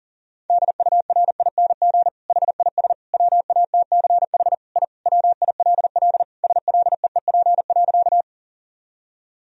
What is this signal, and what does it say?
Morse code, heard as a keyed tone, about 40 wpm, 700 Hz. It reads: DURING HIS WATCH I WILL SLEEP.
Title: Sentences from Top 1000 Words - Review Part 2 - 40wpm